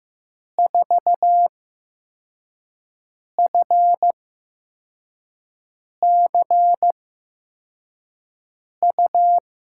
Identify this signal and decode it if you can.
Morse code, heard as a keyed tone, about 15 wpm, spaced as for 4 wpm, 700 Hz. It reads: 4FCU
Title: USA Call Signs - 6 Characters - 8x Character Spacing - 15wpm